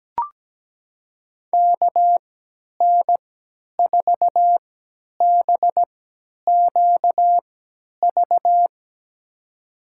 Morse code, heard as a keyed tone, about 17 wpm, spaced as for 10 wpm, 700 Hz. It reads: KN4BQV
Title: USA Call Signs - 3x Character Spacing - 17wpm